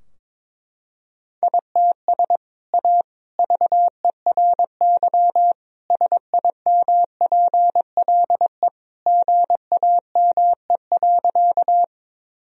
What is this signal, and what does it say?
Morse code, heard as a keyed tone, about 21 wpm, 700 Hz. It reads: ITS A VERY SIMPLE GAME.
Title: Sentences from Top 1000 Words - Review Part 3 - 22wpm